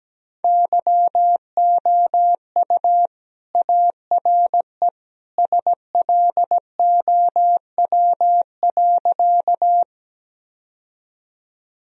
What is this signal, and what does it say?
Morse code, heard as a keyed tone, about 17 wpm, 700 Hz. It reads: YOU ARE SLOW.
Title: Sentences from Top 1000 Words - Review Part 2 - 17wpm